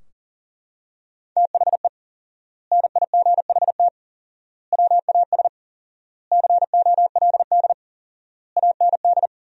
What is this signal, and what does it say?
Morse code, heard as a keyed tone, about 40 wpm, 700 Hz. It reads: THE NIGHT WAS COLD AND
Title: 4x Word-Spacing - Sentences from Top 400 Words - 40wpm